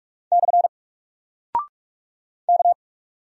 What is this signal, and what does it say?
Morse code, heard as a keyed tone, about 46 wpm, 700 Hz. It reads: / K
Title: Single Character (Letter:Number:Punctuation:Prosign) 45wpm